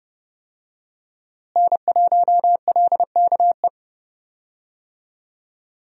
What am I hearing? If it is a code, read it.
Morse code, heard as a keyed tone, about 30 wpm, 700 Hz. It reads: N1LKE
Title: USA Call Signs - 5 Characters - Repeated 2x - 30wpm